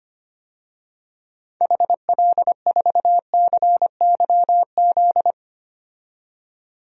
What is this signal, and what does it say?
Morse code, heard as a keyed tone, about 25 wpm, 700 Hz. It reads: HL4CYZ
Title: DXCC-100 Call Signs - 6 Characters - 25wpm